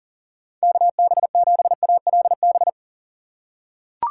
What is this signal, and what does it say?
Morse code, heard as a keyed tone, about 39 wpm, 700 Hz. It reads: KB7ALB
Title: USA Call Signs - 6 Characters - 3x Character Spacing - 40wpm